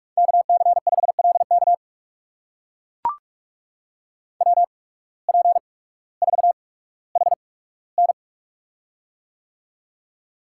Morse code, heard as a keyed tone, about 44 wpm, spaced as for 19 wpm, 700 Hz. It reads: KC5LK WP4HN
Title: USA Call Signs - 5 Characters - 8x Character Spacing - 45wpm